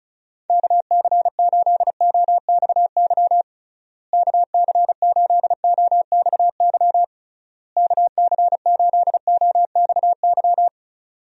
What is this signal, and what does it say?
Morse code, heard as a keyed tone, about 36 wpm, 700 Hz. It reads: KC8OXY KC8OXY KC8OXY
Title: USA Call Signs - 6 Characters - Repeated 3x - 35wpm